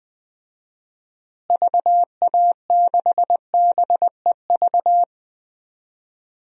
Morse code, heard as a keyed tone, about 20 wpm, 700 Hz. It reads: VA6BEV
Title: DXCC-100 Call Signs - 6 Characters - Repeated 3x - 20wpm